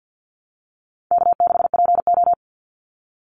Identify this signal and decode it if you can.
Morse code, heard as a keyed tone, about 48 wpm, 700 Hz. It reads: K6PO
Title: Easy Call Signs Repeated 2x - 50wpm